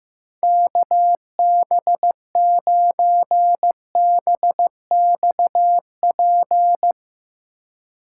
Morse code, heard as a keyed tone, about 15 wpm, 700 Hz. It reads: KB9BXP
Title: USA Call Signs - 6 Characters - 4x Character Spacing - 15wpm